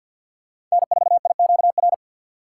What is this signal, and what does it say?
Morse code, heard as a keyed tone, about 53 wpm, 700 Hz. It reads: N4IQR
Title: USA Call Signs - 5 Characters - 4x Character Spacing - 50wpm